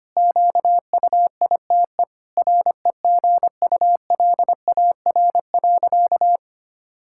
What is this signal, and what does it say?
Morse code, heard as a keyed tone, about 25 wpm, 700 Hz. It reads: QUITE REGULAR.
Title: Sentences from Top 1000 Words - Review Part 6 - 25wpm